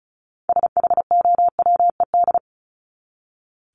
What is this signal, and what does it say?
Morse code, heard as a keyed tone, about 34 wpm, 700 Hz. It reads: SHOWED
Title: ED Suffix - Top 100 Words with Base Words 35wpm